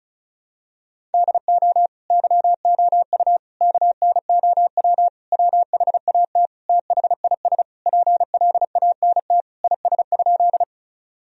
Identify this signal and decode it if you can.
Morse code, heard as a keyed tone, about 36 wpm, 700 Hz. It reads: DO YOU KNOW WHAT THIS PLANT IS?
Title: Sentences from Top 1000 Words - Review Part 1 - 35wpm